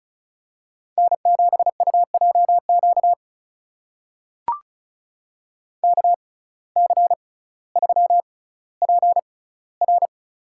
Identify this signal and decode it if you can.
Morse code, heard as a keyed tone, about 36 wpm, spaced as for 18 wpm, 700 Hz. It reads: N7UJQ KC3PR
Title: USA Call Signs - 5 Characters - 6x Character Spacing - 35wpm